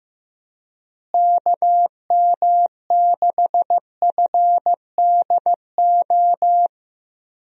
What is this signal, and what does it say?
Morse code, heard as a keyed tone, about 16 wpm, 700 Hz. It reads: KM6FDO
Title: USA Call Signs - 6 Characters - Repeated 2x - 15wpm